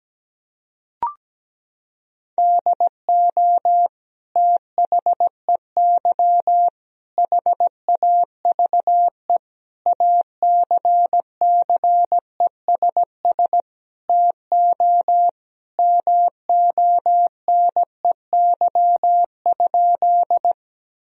Morse code, heard as a keyed tone, about 17 wpm, 700 Hz. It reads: DO THEY HAVE ACCESS TO MONEY?
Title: Sentences from Top 1000 Words - Review Part 4 - 17wpm